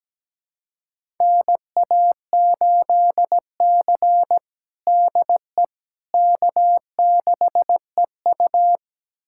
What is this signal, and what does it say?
Morse code, heard as a keyed tone, about 17 wpm, 700 Hz. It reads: NA8C DE K6EU
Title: USA Callsign DE Callsign - 4 Characters - 17wpm